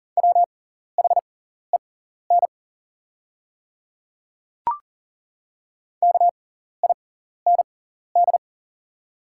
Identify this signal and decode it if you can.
Morse code, heard as a keyed tone, about 40 wpm, spaced as for 14 wpm, 700 Hz. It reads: WHEN KIND
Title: 6x Character Spacing - Top 200 Words - 40wpm